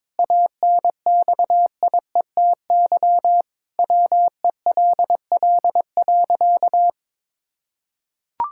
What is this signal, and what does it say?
Morse code, heard as a keyed tone, about 21 wpm, 700 Hz. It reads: ANXIETY WELL.
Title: Sentences from Top 1600 Words 22wpm